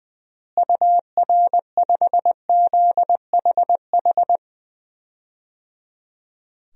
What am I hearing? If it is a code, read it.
Morse code, heard as a keyed tone, about 21 wpm, 700 Hz. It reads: UR5ZHH
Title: DXCC-100 Call Signs - 6 Characters - Repeated 3x - 20wpm